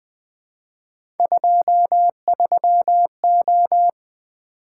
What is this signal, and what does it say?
Morse code, heard as a keyed tone, about 20 wpm, 700 Hz. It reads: 23O
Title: Three Letter-Number Combinations - 20wpm